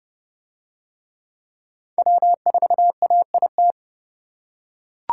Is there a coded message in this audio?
Yes: Morse code from 2.0 to 3.7 s, heard as a keyed tone, about 27 wpm, 700 Hz. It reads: W4AIT